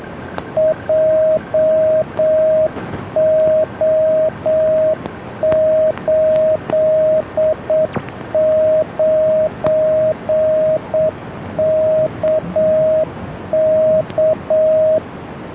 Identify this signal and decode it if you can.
Morse code, heard as a keyed tone, about 7 wpm, 630 Hz, with radio noise. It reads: JO89KK